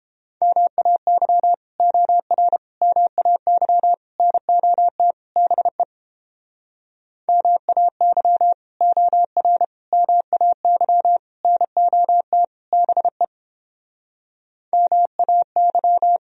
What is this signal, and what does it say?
Morse code, heard as a keyed tone, about 31 wpm, 700 Hz. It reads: MAY OR MAY NOT BE MAY OR MAY NOT BE MAY